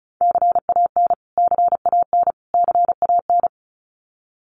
Morse code, heard as a keyed tone, about 34 wpm, 700 Hz. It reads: CAN CAN CAN
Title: Top 200 Words Repeated 35wpm